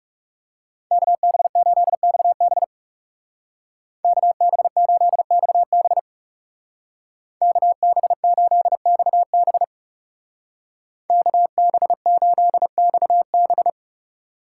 Morse code, heard as a keyed tone, about 39 wpm, 700 Hz. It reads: KB8XB KB8XB KB8XB KB8XB